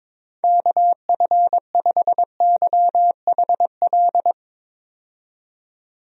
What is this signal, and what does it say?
Morse code, heard as a keyed tone, about 22 wpm, 700 Hz. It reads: KF5YHL
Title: USA Call Signs - 6 Characters - 22wpm